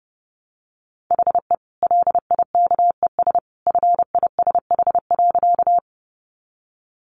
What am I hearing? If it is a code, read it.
Morse code, heard as a keyed tone, about 30 wpm, 700 Hz. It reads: HE LIKES FISH.
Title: Easy Phrases 30wpm